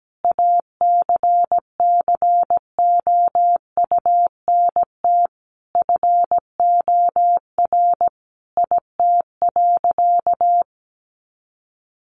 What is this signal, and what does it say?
Morse code, heard as a keyed tone, about 17 wpm, 700 Hz. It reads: ACCOUNT FOR IT.
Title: Sentences from Top 900 Words 17wpm